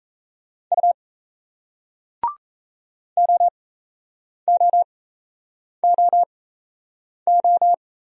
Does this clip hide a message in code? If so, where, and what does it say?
Yes: Morse code from 0.7 to 7.7 s, heard as a keyed tone, about 38 wpm, spaced as for 8 wpm, 700 Hz. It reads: U OOOO